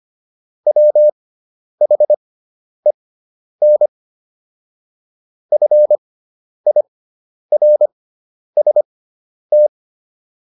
Morse code, heard as a keyed tone, about 25 wpm, spaced as for 10 wpm, 600 Hz. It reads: WHEN FIRST